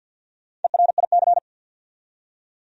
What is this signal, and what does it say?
Morse code, heard as a keyed tone, about 52 wpm, 700 Hz. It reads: ERIC
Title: USA Common Names - Max 4 Letters - 50wpm